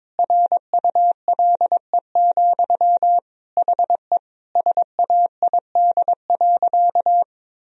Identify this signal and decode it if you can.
Morse code, heard as a keyed tone, about 22 wpm, 700 Hz. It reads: RULE, HE SAID.
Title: Sentences from Top 1000 Words - Review Part 2 - 22wpm